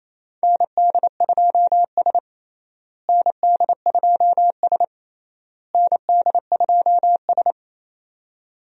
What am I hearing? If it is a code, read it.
Morse code, heard as a keyed tone, about 27 wpm, 700 Hz. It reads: ND2S ND2S ND2S
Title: USA Call Signs - 4 Characters - Repeated 3x - 28wpm